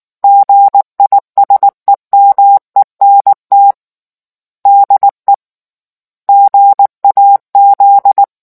Morse code, heard as a keyed tone, about 19 wpm, 800 Hz. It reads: GISEMENT DE GAZ